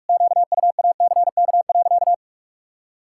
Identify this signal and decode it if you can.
Morse code, heard as a keyed tone, about 41 wpm, 700 Hz. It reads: QUACK.
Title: Pangram Challenge v2 - 45wpm